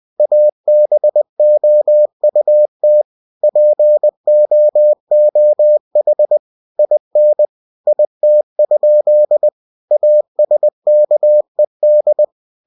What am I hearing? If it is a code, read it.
Morse code, heard as a keyed tone, about 20 wpm, 600 Hz. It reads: ABOUT POOH IN IT? ASKED
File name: cw-books / winnie-the-pooh / winnie-20wpm-page-0098